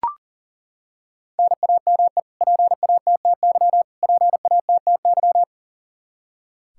Morse code, heard as a keyed tone, about 40 wpm, 700 Hz. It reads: NAME PATTY PATTY
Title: Practice QSO - Exchange 1 - Beginning - 40wpm